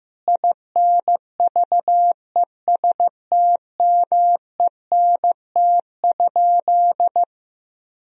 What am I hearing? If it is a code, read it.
Morse code, heard as a keyed tone, about 16 wpm, 700 Hz. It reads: INVESTMENT?
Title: Sentences from Top 1000 Words - Review Part 4 - 15wpm